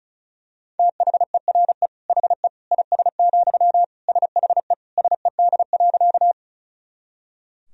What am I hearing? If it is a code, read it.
Morse code, heard as a keyed tone, about 36 wpm, 700 Hz. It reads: THERE HE IS, SHE SED.
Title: Sentences from Top 100 Words and Abbreviations 35wpm